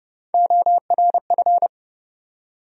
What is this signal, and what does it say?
Morse code, heard as a keyed tone, about 32 wpm, 700 Hz. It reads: ORF
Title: USA Call Sign Postfixes - 3 Letters 30wpm